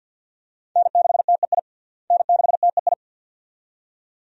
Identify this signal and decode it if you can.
Morse code, heard as a keyed tone, about 52 wpm, 700 Hz. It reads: N6TEI N6TEI
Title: USA Call Signs - 5 Characters - Repeated 2x - 50wpm